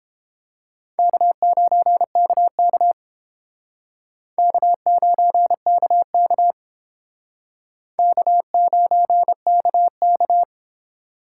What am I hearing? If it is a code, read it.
Morse code, heard as a keyed tone, about 28 wpm, 700 Hz. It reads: K9KK K9KK K9KK